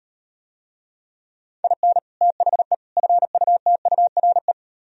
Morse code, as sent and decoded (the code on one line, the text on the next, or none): .. -. / - .... / ..-. ..- - ..- .-. .
IN THE FUTURE